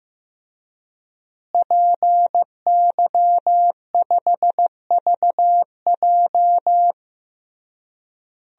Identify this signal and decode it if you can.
Morse code, heard as a keyed tone, about 15 wpm, 700 Hz. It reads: PY5VJ
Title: DXCC-100 Call Signs - 5 Characters - Repeated 3x - 15wpm